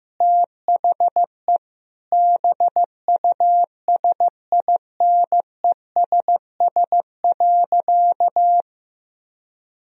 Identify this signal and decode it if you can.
Morse code, heard as a keyed tone, about 16 wpm, 700 Hz. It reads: THE BUSINESS.
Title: Sentences from Top 1000 Words - Review Part 3 - 15wpm